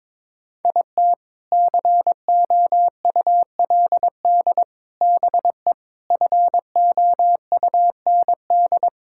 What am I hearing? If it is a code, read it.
Morse code, heard as a keyed tone, about 22 wpm, 700 Hz. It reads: IT COULD BE FOUND
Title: Sentences from Top 1000 Words - Review Part 5 - 22wpm